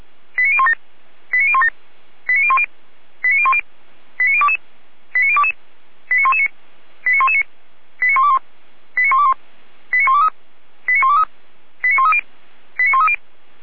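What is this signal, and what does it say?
Five-tone selective call (ZVEI), heard as quick runs of five tones, tones of about 70 ms. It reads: 89017 89017 89010 89010 89022 89022 89119 89119 89121 89121 89123 89123 89130 89130
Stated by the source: ZVEI five-tone selective call received on SDR.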